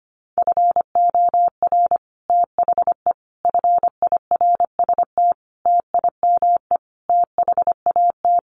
Morse code, heard as a keyed tone, about 25 wpm, 700 Hz. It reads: FOR THE FIRST TIME THAT
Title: Sets of 5 Words Encore 25wpm